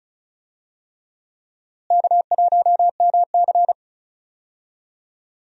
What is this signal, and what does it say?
Morse code, heard as a keyed tone, about 36 wpm, 700 Hz. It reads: K1MC
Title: USA Call Signs - 4 Characters - Repeated 3x - 35wpm